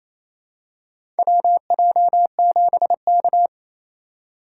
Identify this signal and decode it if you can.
Morse code, heard as a keyed tone, about 28 wpm, 700 Hz. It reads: WJ7K
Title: USA Call Signs - 2x Character Spacing - 28wpm